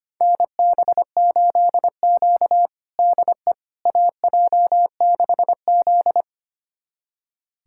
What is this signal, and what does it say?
Morse code, heard as a keyed tone, about 25 wpm, 700 Hz. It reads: NB8Q DE AJ6Z
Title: USA Callsign DE Callsign - 4 Characters - 25wpm